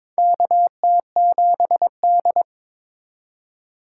Morse code, heard as a keyed tone, about 22 wpm, 700 Hz. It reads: KT7D
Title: USA Call Signs - 4 Characters - 4x Character Spacing - 22wpm